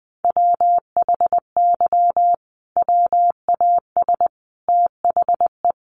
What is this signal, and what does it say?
Morse code, heard as a keyed tone, about 19 wpm, 700 Hz. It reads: WHY WAS THE
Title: Sentences from Top 1400 Words 20wpm